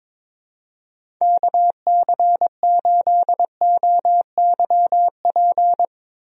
Morse code, heard as a keyed tone, about 22 wpm, 700 Hz. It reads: KC8OYP
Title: USA Call Signs - 6 Characters - Repeated 2x - 22wpm